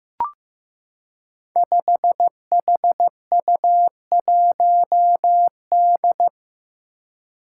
Morse code, heard as a keyed tone, about 15 wpm, 700 Hz. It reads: 5HU1D
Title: Five Challenging Letter-Number Combinations 15wpm